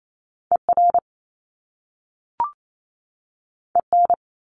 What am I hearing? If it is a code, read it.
Morse code, heard as a keyed tone, about 28 wpm, 700 Hz. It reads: ER EN
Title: Top Two Letter Combinations in Words 28wpm